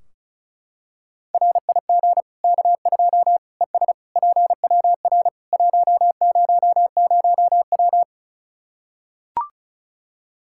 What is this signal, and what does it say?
Morse code, heard as a keyed tone, about 36 wpm, 700 Hz. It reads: RIG K2 ES PWR 100W
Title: Practice QSO - Exchange 2 - Info - 35wpm